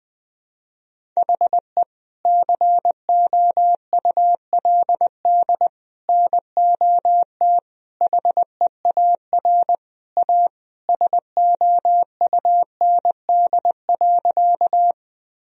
Morse code, heard as a keyed tone, about 20 wpm, 700 Hz. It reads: HE COULD NOT HEAR A SOUND.
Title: Sentences from Top 1000 Words - Review Part 1 - 20wpm